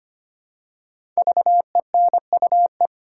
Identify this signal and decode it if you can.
Morse code, heard as a keyed tone, about 25 wpm, 700 Hz. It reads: VENUE
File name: 5 Letter Words - First 3 Letters Given - 25wpm